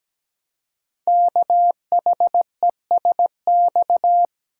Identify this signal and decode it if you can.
Morse code, heard as a keyed tone, about 17 wpm, 700 Hz. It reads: KHESX
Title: Code Groups - 5 Letters - 17wpm